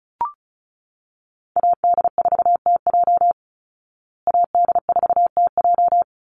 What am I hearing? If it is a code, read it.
Morse code, heard as a keyed tone, about 34 wpm, 700 Hz. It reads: AD4TJ AD4TJ
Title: Call Signs Repeated 3x - 35wpm